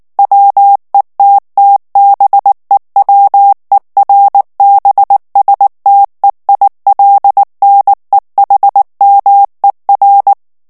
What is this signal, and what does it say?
Morse code, heard as a keyed tone, about 19 wpm, 800 Hz. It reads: WETTBEWERBSTEILNEHMER
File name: morse mp3 / ALLEMAND texte clair 2 vitesse 18 mots minute